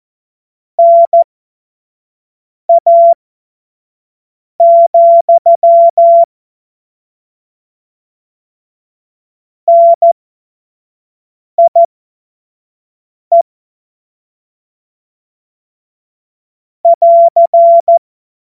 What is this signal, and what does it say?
Morse code, heard as a keyed tone, about 14 wpm, spaced as for 5 wpm, 680 Hz. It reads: NA, NIE <AR>